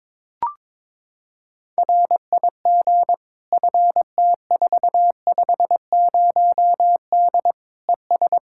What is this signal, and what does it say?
Morse code, heard as a keyed tone, about 22 wpm, 700 Hz. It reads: RIG FT450D ES
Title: K1USN SST QSO Practice 22wpm